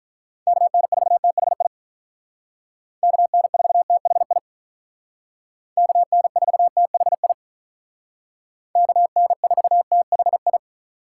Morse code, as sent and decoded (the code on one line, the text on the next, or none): -.- -. ....- - .... .. / -.- -. ....- - .... .. / -.- -. ....- - .... .. / -.- -. ....- - .... ..
KN4THI KN4THI KN4THI KN4THI